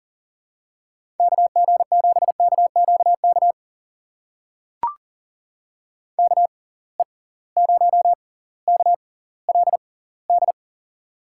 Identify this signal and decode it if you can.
Morse code, heard as a keyed tone, about 38 wpm, spaced as for 20 wpm, 700 Hz. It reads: KG7KQK KE0KLD
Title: USA Call Signs - 6 Characters - 6x Character Spacing - 40wpm